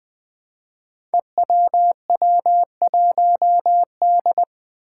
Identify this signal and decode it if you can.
Morse code, heard as a keyed tone, about 21 wpm, 700 Hz. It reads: EWW1D
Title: Five Challenging Letter-Number Combinations 20wpm